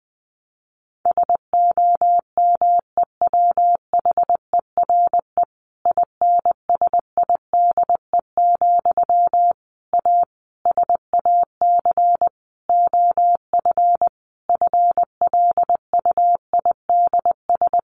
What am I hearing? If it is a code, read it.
Morse code, heard as a keyed tone, about 20 wpm, 700 Hz. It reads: SOMEWHERE INSIDE, A SAC OF FLUIDS
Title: A Sound Of Thunder 20wpm